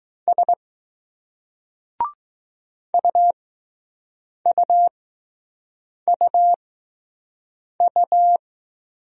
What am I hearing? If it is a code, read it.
Morse code, heard as a keyed tone, about 20 wpm, spaced as for 6 wpm, 700 Hz. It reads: S UUUU